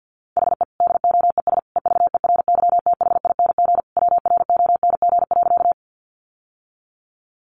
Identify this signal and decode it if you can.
Morse code, heard as a keyed tone, about 50 wpm, 700 Hz. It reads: HE DOES EVERYTHING WRONG.
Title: Sentences from Top 900 Words 50wpm